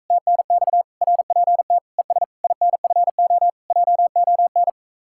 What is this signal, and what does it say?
Morse code, heard as a keyed tone, about 46 wpm, 700 Hz. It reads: TNX RPT ES INFO JON